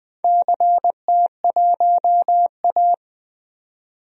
Morse code, heard as a keyed tone, about 20 wpm, 700 Hz. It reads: CT1A
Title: DXCC-100 Call Signs - 4 Characters - 20wpm